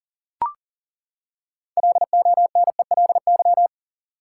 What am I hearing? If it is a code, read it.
Morse code, heard as a keyed tone, about 40 wpm, 700 Hz. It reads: LONELY
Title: Six Letter Words - 40wpm